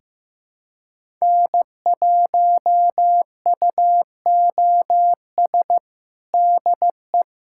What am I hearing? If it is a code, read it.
Morse code, heard as a keyed tone, about 15 wpm, 700 Hz. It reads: N1UOS DE
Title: USA Callsign DE Callsign 15wpm